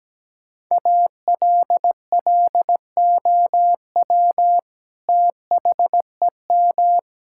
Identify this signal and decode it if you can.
Morse code, heard as a keyed tone, about 17 wpm, 700 Hz. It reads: ALLOW THEM
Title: Sentences from Top 1000 Words - Review Part 5 - 17wpm